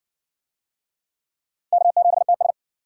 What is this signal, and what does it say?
Morse code, heard as a keyed tone, about 59 wpm, 700 Hz. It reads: K7TS